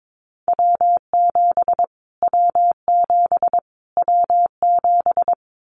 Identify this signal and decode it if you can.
Morse code, heard as a keyed tone, about 22 wpm, 700 Hz. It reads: W7 W7 W7
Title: Callsign Prefixes for Top 25 DXCC - Repeated 3x - 22wpm